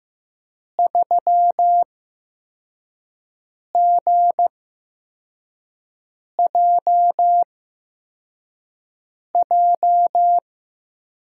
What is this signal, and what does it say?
Morse code, heard as a keyed tone, about 15 wpm, spaced as for 4 wpm, 700 Hz. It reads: 3GJJ
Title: USA Call Signs - 6 Characters - 8x Character Spacing - 15wpm